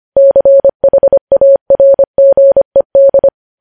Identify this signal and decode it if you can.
Morse code, heard as a keyed tone, about 25 wpm, 550 Hz. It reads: CHARGED
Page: news headlines in morse code at 25 wpm